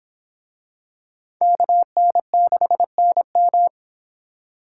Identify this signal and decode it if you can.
Morse code, heard as a keyed tone, about 24 wpm, 700 Hz. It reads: KN6NM